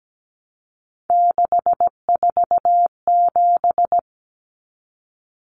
Morse code, heard as a keyed tone, about 17 wpm, 700 Hz. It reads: 647
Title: Numbers - 3-Digit 17wpm